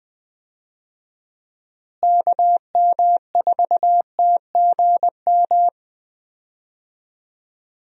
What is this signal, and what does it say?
Morse code, heard as a keyed tone, about 21 wpm, 700 Hz. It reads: KM4TGM
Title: USA Call Signs - 6 Characters - Repeated 2x - 20wpm